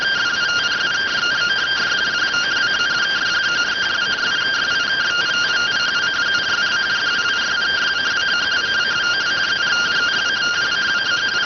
Two-tone teleprinter signal, about 170 Hz shift.